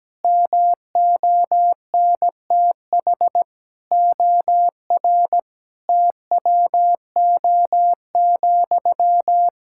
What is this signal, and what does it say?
Morse code, heard as a keyed tone, about 17 wpm, 700 Hz. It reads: MONTH OR TWO,
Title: Sentences from Top 1000 Words - Review Part 5 - 17wpm